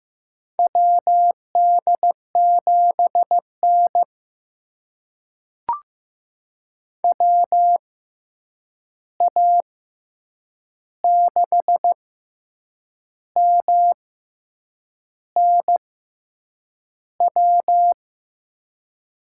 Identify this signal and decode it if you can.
Morse code, heard as a keyed tone, about 15 wpm, spaced as for 7 wpm, 700 Hz. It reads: WD7N WA6MNW
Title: USA Call Signs - 6x Character Spacing - 15wpm